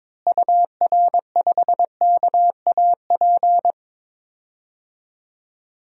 Morse code, heard as a keyed tone, about 22 wpm, 700 Hz. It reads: UR5KAP
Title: DXCC-100 Call Signs - 6 Characters - Repeated 3x - 22wpm